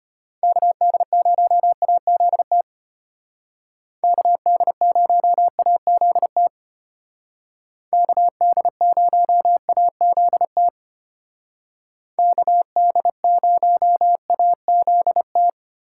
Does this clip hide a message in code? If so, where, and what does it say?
Yes: Morse code from 0.4 to 15.5 s, heard as a keyed tone, about 32 wpm, 700 Hz. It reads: KD0AZT KD0AZT KD0AZT KD0AZT